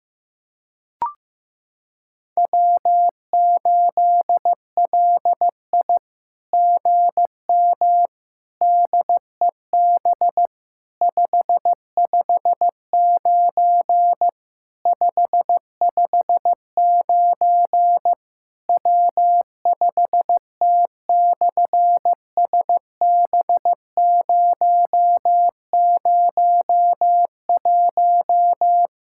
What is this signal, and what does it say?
Morse code, heard as a keyed tone, about 15 wpm, 700 Hz. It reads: W8LI GM DEB 559 559 W5T/SB001
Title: SOTA QSO Practice 15wpm